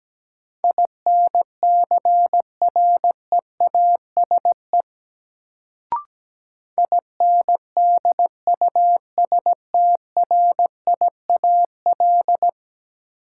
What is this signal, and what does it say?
Morse code, heard as a keyed tone, about 17 wpm, 700 Hz. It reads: INCREASE INDUSTRIAL
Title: Sets of 5 Words - New Word Drill - 17wpm